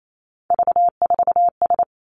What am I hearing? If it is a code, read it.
Morse code, heard as a keyed tone, about 28 wpm, 700 Hz. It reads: V4S
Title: Three Challenging Letter-Number Combinations 28wpm